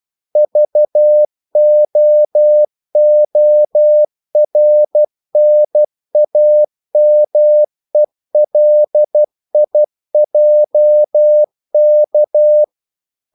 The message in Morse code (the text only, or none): VOORNAMELIJK